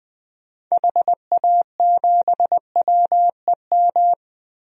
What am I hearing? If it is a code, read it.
Morse code, heard as a keyed tone, about 20 wpm, 700 Hz. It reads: HA7WEM
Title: DXCC-100 Call Signs - 6 Characters - 20wpm